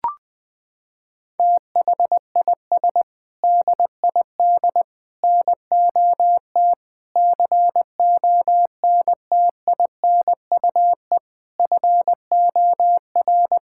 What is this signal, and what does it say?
Morse code, heard as a keyed tone, about 20 wpm, 700 Hz. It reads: THIS DID NOT CONTINUE FOR
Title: Sentences from Top 1000 Words - Review Part 6 - 20wpm